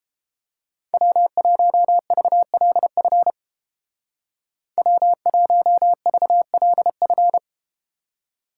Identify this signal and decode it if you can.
Morse code, heard as a keyed tone, about 32 wpm, 690 Hz. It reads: W1VLF W1VLF